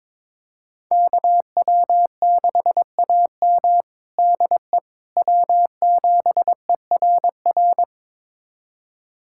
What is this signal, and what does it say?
Morse code, heard as a keyed tone, about 22 wpm, 700 Hz. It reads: KW6AM DE W7ERR
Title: USA Callsign DE Callsign - 5 Characters - 22wpm